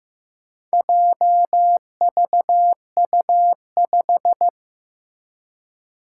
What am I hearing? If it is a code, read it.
Morse code, heard as a keyed tone, about 15 wpm, 700 Hz. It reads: JVU5
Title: Four Challenging Letter-Number Combinations 15wpm